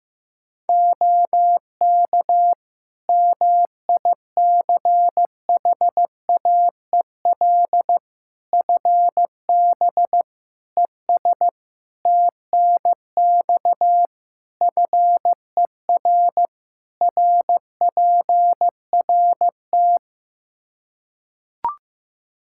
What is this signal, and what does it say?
Morse code, heard as a keyed tone, about 15 wpm, 700 Hz. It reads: OK MICHAEL FB ES TNX FER RPRT
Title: K1USN SST QSO Practice 15wpm